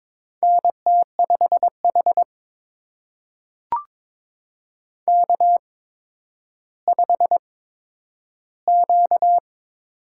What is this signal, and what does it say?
Morse code, heard as a keyed tone, about 22 wpm, spaced as for 11 wpm, 700 Hz. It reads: NT5H K5Q
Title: USA Call Signs - 8x Character Spacing - 22wpm